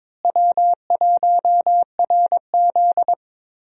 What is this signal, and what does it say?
Morse code, heard as a keyed tone, about 21 wpm, 700 Hz. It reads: W1RZ
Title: USA Call Signs - 4 Characters - Repeated 2x - 22wpm